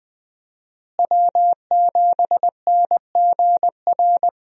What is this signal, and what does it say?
Morse code, heard as a keyed tone, about 20 wpm, 700 Hz. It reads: W7NGR
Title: USA Call Signs - 5 Characters - Repeated 2x - 20wpm